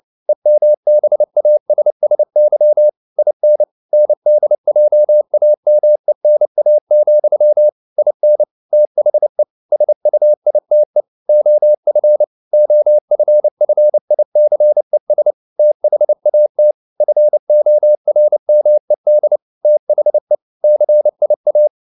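Morse code, heard as a keyed tone, about 29 wpm, 600 Hz. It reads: EMBASSY IN NDJAMENA, IN THE SUITE OF OFFICES THAT FORMED THE CIA
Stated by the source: Morse code at 29 wpm, sending Ken Follett's novel Never in English, from CW med Gnister.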